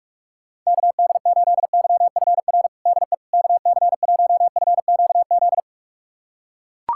Morse code, heard as a keyed tone, about 42 wpm, 700 Hz. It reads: KD8YFR DE KC1FQZ
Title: USA Callsign DE Callsign 45wpm